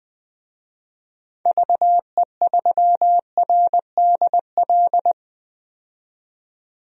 Morse code, heard as a keyed tone, about 20 wpm, 700 Hz. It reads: VE3RDL
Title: DXCC-100 Call Signs - 6 Characters - 20wpm